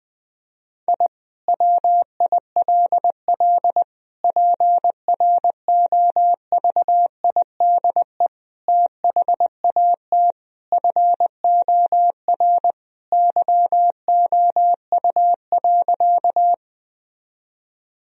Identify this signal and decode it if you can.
Morse code, heard as a keyed tone, about 21 wpm, 700 Hz. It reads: I WILL PROVIDE THAT FOR YOU.
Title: Sentences from Top 1000 Words - Review Part 3 - 20wpm